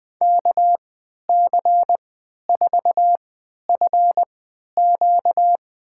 Morse code, heard as a keyed tone, about 20 wpm, spaced as for 11 wpm, 700 Hz. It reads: KC4FQ
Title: USA Call Signs - 3x Character Spacing - 20wpm